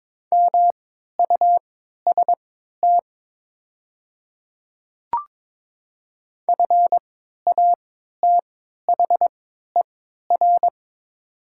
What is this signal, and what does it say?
Morse code, heard as a keyed tone, about 22 wpm, spaced as for 13 wpm, 700 Hz. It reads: MUST FATHER
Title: 3x Character Spacing - Top 300 Words - 22wpm